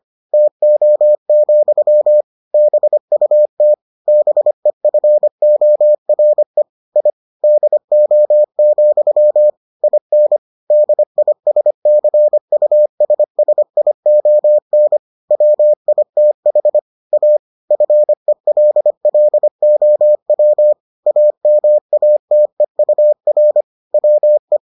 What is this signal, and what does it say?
Morse code, heard as a keyed tone, about 25 wpm, 600 Hz. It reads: TO, BUT BEFORE I DO, IN DISCUSSION WITH A FELLOW AMATEUR WE